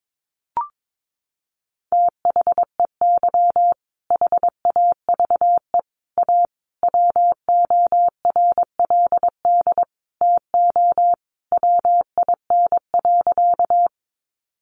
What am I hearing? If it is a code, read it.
Morse code, heard as a keyed tone, about 22 wpm, 700 Hz. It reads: THEY HAVE A WORLD TO WIN.
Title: Sentences from Top 1500 Words 22wpm